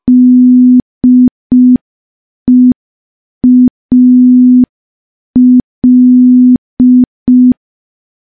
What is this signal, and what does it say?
Morse code, heard as a keyed tone, about 5 wpm, 250 Hz. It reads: DEAL
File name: morse / morse-training-05WPM